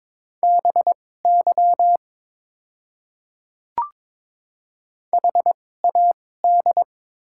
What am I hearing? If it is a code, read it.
Morse code, heard as a keyed tone, about 22 wpm, spaced as for 16 wpm, 700 Hz. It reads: BY HAD
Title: 2x Character Spacing - Top 200 Words - 22wpm